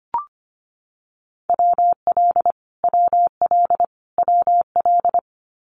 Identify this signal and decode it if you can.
Morse code, heard as a keyed tone, about 25 wpm, 700 Hz. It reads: WL WL WL
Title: Abbreviations Repeated 3x - 25wpm